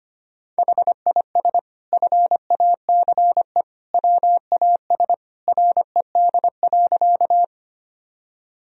Morse code, heard as a keyed tone, about 25 wpm, 700 Hz. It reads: HIS FACE WAS RED.
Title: Sentences from Top 1000 Words - Review Part 2 - 25wpm